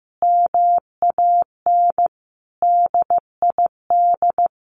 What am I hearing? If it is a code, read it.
Morse code, heard as a keyed tone, about 15 wpm, 700 Hz. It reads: MAN DID
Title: A Sound Of Thunder 15wpm